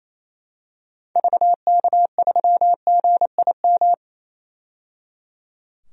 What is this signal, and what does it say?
Morse code, heard as a keyed tone, about 27 wpm, 700 Hz. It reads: VK3GIM
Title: DXCC-100 Call Signs - 6 Characters - 28wpm